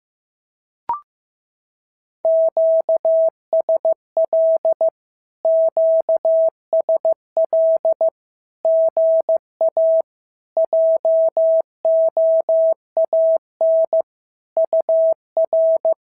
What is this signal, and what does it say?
Morse code, heard as a keyed tone, about 15 wpm, 650 Hz. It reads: QSL QSL GA JOAN UR
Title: SOTA QSO Practice 15wpm